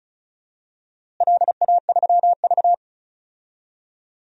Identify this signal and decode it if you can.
Morse code, heard as a keyed tone, about 34 wpm, 700 Hz. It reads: LA3V
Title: DXCC-100 Call Signs - 4 Characters - Repeated 3x - 35wpm